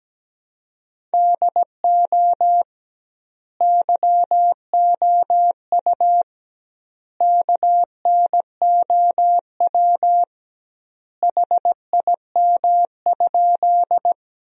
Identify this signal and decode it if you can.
Morse code, heard as a keyed tone, about 17 wpm, 700 Hz. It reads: DO YOU KNOW HIM?
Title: 2x Word-Spacing - Sentences from Top 100 Words - 17wpm